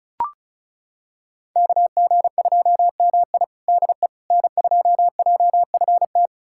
Morse code, heard as a keyed tone, about 36 wpm, 700 Hz. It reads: KG2MI DE N2JFT
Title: USA Callsign DE Callsign 35wpm